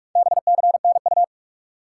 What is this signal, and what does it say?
Morse code, heard as a keyed tone, about 46 wpm, 700 Hz. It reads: BCNU
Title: Top 180 QSO Elements 45wpm